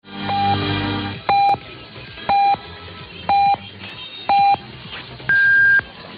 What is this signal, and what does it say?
Radio time signal (time pips). It is heard from 0.3 to 5.0 s.
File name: Chinese time signal and announcement 1600 UTC